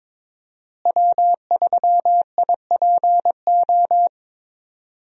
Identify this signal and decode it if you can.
Morse code, heard as a keyed tone, about 22 wpm, 700 Hz. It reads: W3IPO
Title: USA Call Signs - 5 Characters - Repeated 3x - 22wpm